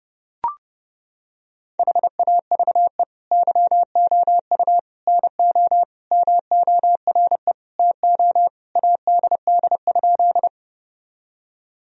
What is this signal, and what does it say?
Morse code, heard as a keyed tone, about 30 wpm, 700 Hz. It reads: HAVE YOU NO MORE TO ADD?
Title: Sentences from Top 1000 Words - Review Part 1 - 30wpm